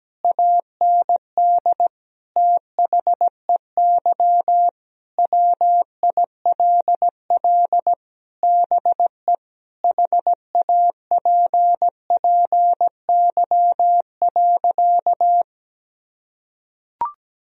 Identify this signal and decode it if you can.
Morse code, heard as a keyed tone, about 17 wpm, 700 Hz. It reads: AND THEY WILL BE HAPPY.
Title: Sentences from Top 1000 Words - Review Part 2 - 17wpm